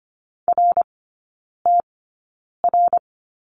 Morse code, heard as a keyed tone, about 25 wpm, spaced as for 9 wpm, 700 Hz. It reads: RTR
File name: Lesson 15e - Number 5 - Three Characters 25wpm